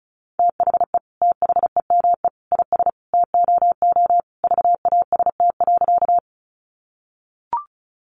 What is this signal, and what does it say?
Morse code, heard as a keyed tone, about 34 wpm, 700 Hz. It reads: THE THEME IS TOO VAST.
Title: Sentences from Top 1300 Words 35wpm